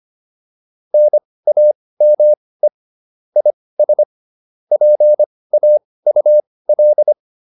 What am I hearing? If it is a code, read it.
Morse code, heard as a keyed tone, about 25 wpm, spaced as for 18 wpm, 600 Hz. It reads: NAME IS PAUL
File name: QSO403 18